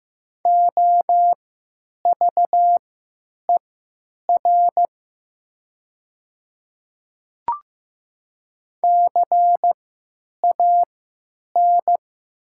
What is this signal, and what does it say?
Morse code, heard as a keyed tone, about 16 wpm, spaced as for 9 wpm, 700 Hz. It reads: OVER CAN
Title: 3x Character Spacing - Top 100 Words - 15wpm